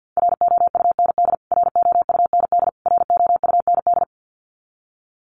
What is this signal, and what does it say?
Morse code, heard as a keyed tone, about 49 wpm, 700 Hz. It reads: ROUND ROUND ROUND
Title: Top 300 Words Repeated 50wpm